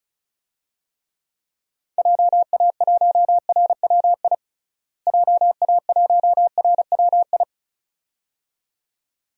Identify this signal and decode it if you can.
Morse code, heard as a keyed tone, about 36 wpm, 700 Hz. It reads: JA1RWI JA1RWI
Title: USA Call Signs - 6 Characters - Repeated 2x - 35wpm